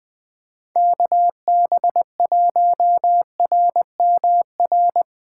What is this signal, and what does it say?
Morse code, heard as a keyed tone, about 19 wpm, 700 Hz. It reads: KB1RMR
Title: Speed Racing - USA 2x3 Call Signs 17wpm